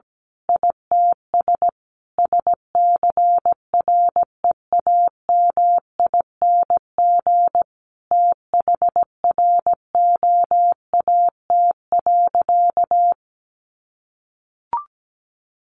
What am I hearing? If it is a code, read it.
Morse code, heard as a keyed tone, about 17 wpm, 700 Hz. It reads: ITS SCREAMING THROAT.
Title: A Sound Of Thunder 17wpm